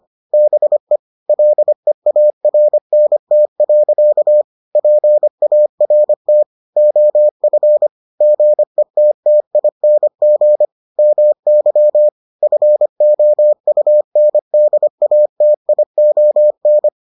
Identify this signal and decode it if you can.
Morse code, heard as a keyed tone, about 25 wpm, 600 Hz. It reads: BE LEARNT. PART OF GETTING MY FOUNDATION